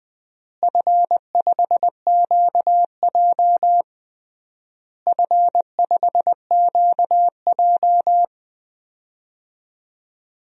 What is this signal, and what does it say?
Morse code, heard as a keyed tone, about 20 wpm, 700 Hz. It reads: F5QJ F5QJ
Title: DXCC-100 Call Signs - 4 Characters - Repeated 3x - 20wpm